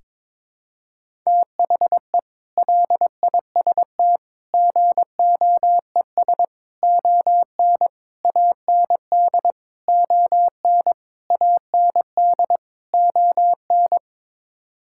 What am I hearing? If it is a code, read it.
Morse code, heard as a keyed tone, about 22 wpm, 700 Hz. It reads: THE LIST GOES ON AND ON AND ON
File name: practice / 22wpm / Sets of 8 Words - 22wpm